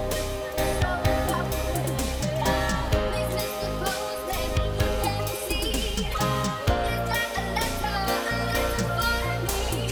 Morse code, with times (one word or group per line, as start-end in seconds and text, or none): none